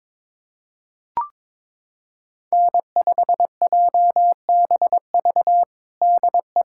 Morse code, heard as a keyed tone, about 22 wpm, 700 Hz. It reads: N5JBV DE
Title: USA Callsign DE Callsign - 5 Characters - 22wpm